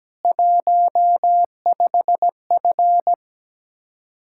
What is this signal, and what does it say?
Morse code, heard as a keyed tone, about 17 wpm, 700 Hz. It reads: 15F
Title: Speed-Racing Common QSO Phrases Encore - 17wpm